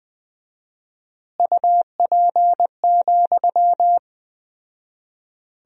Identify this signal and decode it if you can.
Morse code, heard as a keyed tone, about 21 wpm, 700 Hz. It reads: UP,
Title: 4x Word-Spacing - Sentences from Top 500 Words - 20wpm